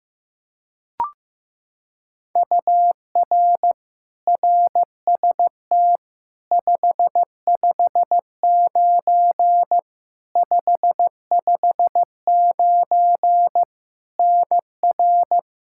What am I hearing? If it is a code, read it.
Morse code, heard as a keyed tone, about 16 wpm, 700 Hz. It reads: UR RST 559 559 NR
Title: Practice QSO - Exchange 1 - Beginning - 15wpm